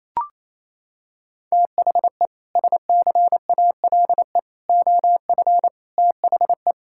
Morse code, heard as a keyed tone, about 28 wpm, 700 Hz. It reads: THE SCALE OF THE
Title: Sentences from Top 1000 Words - Review Part 5 - 28wpm